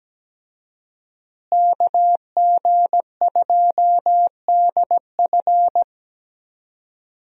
Morse code, heard as a keyed tone, about 17 wpm, 700 Hz. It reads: KG2DF
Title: USA Call Signs - 5 Characters - Repeated 3x - 17wpm